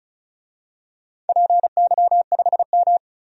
Morse code, heard as a keyed tone, about 36 wpm, 700 Hz. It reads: PY5M
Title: DXCC-100 Call Signs - 4 Characters - 35wpm